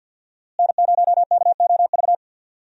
Morse code, heard as a keyed tone, about 52 wpm, 700 Hz. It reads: N0KOV
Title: USA Call Signs - 5 Characters - Repeated 3x - 50wpm